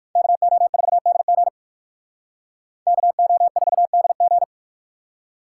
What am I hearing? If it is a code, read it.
Morse code, heard as a keyed tone, about 58 wpm, 690 Hz. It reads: KO4DG KO4DG